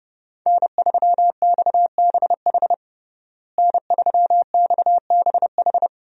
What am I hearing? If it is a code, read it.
Morse code, heard as a keyed tone, about 32 wpm, 700 Hz. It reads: N3XBH N3XBH
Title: USA Call Signs - 5 Characters - Repeated 2x - 30wpm